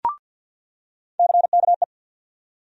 Morse code, heard as a keyed tone, about 48 wpm, 700 Hz. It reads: CKE
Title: USA Call Sign Postfixes - 3 Letters 50wpm